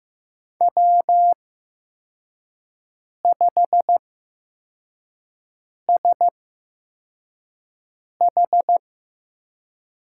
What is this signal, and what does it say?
Morse code, heard as a keyed tone, about 15 wpm, spaced as for 4 wpm, 700 Hz. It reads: W5SH
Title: USA Call Signs - 5 Characters - 8x Character Spacing - 15wpm